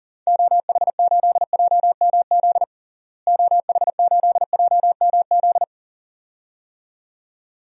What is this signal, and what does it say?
Morse code, heard as a keyed tone, about 37 wpm, 700 Hz. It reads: OH8JMZ OH8JMZ
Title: DXCC-100 Call Signs - 6 Characters - Repeated 2x - 40wpm